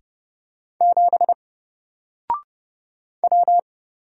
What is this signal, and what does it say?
Morse code, heard as a keyed tone, about 30 wpm, 700 Hz. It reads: 7 W
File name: Single-Letter-Number-30wpm